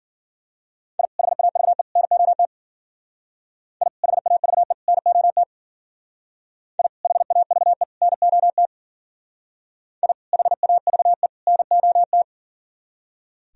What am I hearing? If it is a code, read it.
Morse code, heard as a keyed tone, about 52 wpm, 700 Hz. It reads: I HAVE NOT I HAVE NOT I HAVE NOT I HAVE NOT